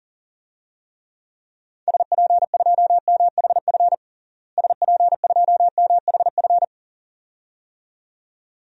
Morse code, heard as a keyed tone, about 40 wpm, 700 Hz. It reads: SP2MHF SP2MHF
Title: DXCC-100 Call Signs - 6 Characters - Repeated 2x - 40wpm